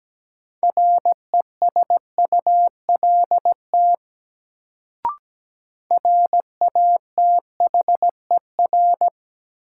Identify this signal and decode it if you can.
Morse code, heard as a keyed tone, about 17 wpm, 700 Hz. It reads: RESULT RATHER
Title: Sentences from Top 700 Words - New Word Drill - 17wpm